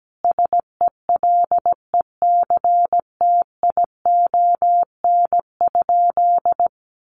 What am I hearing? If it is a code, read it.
Morse code, heard as a keyed tone, about 17 wpm, 700 Hz. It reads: SELECTION?
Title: Sentences from Top 1400 Words 17wpm